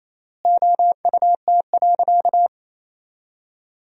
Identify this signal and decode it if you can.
Morse code, heard as a keyed tone, about 28 wpm, 700 Hz. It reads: OUT.
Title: 3x Word-Spacing - Sentences from Top 500 Words - 28wpm